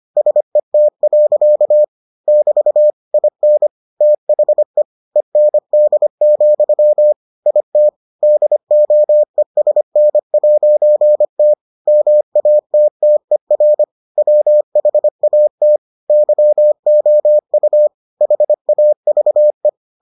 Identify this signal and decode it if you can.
Morse code, heard as a keyed tone, about 25 wpm, 600 Hz. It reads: SET. <BT> IN THE END, IT DOESN'T MATTER WHAT YOU HAVE